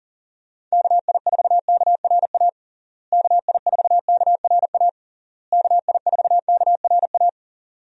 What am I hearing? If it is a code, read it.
Morse code, heard as a keyed tone, about 40 wpm, 700 Hz. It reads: KI4KRA KI4KRA KI4KRA
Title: USA Call Signs - 6 Characters - Repeated 3x - 40wpm